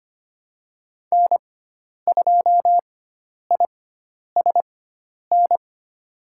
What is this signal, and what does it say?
Morse code, heard as a keyed tone, about 25 wpm, spaced as for 10 wpm, 700 Hz. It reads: N2ISN
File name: Lesson 16g - Letter c - Call Signs 25wpm